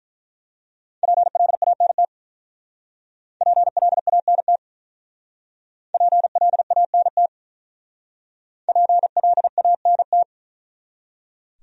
Morse code, heard as a keyed tone, about 43 wpm, 700 Hz. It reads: PLANT PLANT PLANT PLANT
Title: Speed-Racing Top 300 Words 35wpm